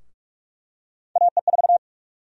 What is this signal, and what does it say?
Morse code, heard as a keyed tone, about 46 wpm, 700 Hz. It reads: AE4
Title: USA Call Sign Prefixes - Repeated 2x - 45wpm